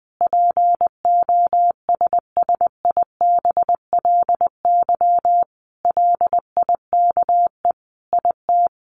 Morse code, heard as a keyed tone, about 19 wpm, 700 Hz. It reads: POSSIBLY LIKE IT
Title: Sentences from Top 1300 Words 20wpm